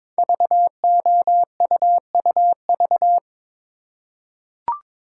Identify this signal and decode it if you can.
Morse code, heard as a keyed tone, about 22 wpm, 700 Hz. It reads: VOUUV
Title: Code Groups - 5 Letters - 22wpm